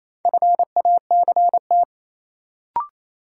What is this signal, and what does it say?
Morse code, heard as a keyed tone, about 28 wpm, 700 Hz. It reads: FACT
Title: Sentences from Top 500 Words - New Word Drill - 28wpm